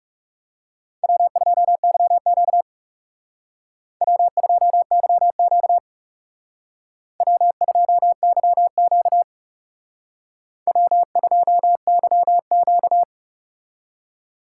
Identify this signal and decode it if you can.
Morse code, heard as a keyed tone, about 38 wpm, 700 Hz. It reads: W2YQ W2YQ W2YQ W2YQ